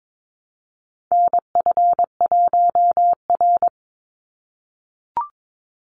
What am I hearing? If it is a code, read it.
Morse code, heard as a keyed tone, about 22 wpm, 700 Hz. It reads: NF1R
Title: NA SOTA Chaser Callsigns 22wpm